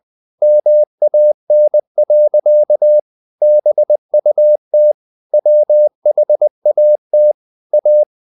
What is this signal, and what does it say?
Morse code, heard as a keyed tone, about 20 wpm, 600 Hz. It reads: MAN. BUT WHAT A
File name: the old man and the sea / the old man and the sea part 1 20wpm